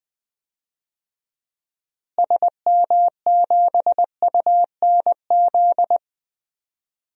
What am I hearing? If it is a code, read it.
Morse code, heard as a keyed tone, about 20 wpm, 700 Hz. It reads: SM7UNZ
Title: DXCC-100 Call Signs - 6 Characters - Repeated 2x - 20wpm